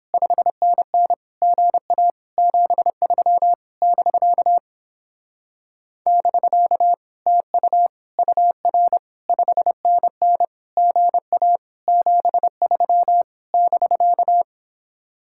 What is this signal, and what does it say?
Morse code, heard as a keyed tone, about 29 wpm, 700 Hz. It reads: 5NN GA 73 <BK> <BK> TU UR 5NN GA 73 <BK>